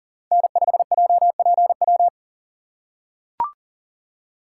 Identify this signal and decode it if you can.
Morse code, heard as a keyed tone, about 40 wpm, 700 Hz. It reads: N5JPW
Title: USA Call Signs - 5 Characters - 4x Character Spacing - 40wpm